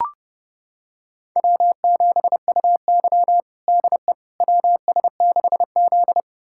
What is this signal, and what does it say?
Morse code, heard as a keyed tone, about 30 wpm, 700 Hz. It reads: W7UY DE WS6Z
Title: USA Callsign DE Callsign - 4 Characters - 30wpm